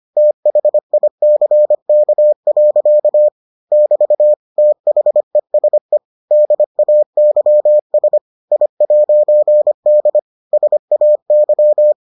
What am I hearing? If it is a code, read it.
Morse code, heard as a keyed tone, about 25 wpm, 600 Hz. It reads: THICK. <BT> THESE DAYS I'D SAY